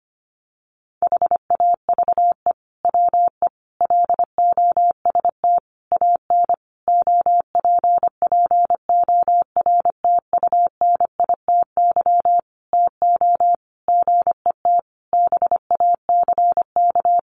Morse code, heard as a keyed tone, about 25 wpm, 700 Hz. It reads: HAVE WE LOST AN OPPORTUNITY TO GET BACK
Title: Sentences from Top 1400 Words 25wpm